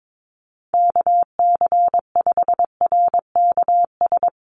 Morse code, heard as a keyed tone, about 22 wpm, 700 Hz. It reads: KC5RKS
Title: USA Call Signs 22wpm